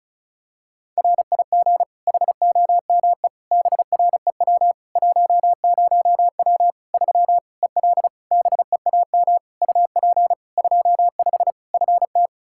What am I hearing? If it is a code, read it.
Morse code, heard as a keyed tone, about 36 wpm, 700 Hz. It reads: RIG HOME BREW 10W 3 EL BEAM UP 25 FT
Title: K1USN SST QSO Practice 35wpm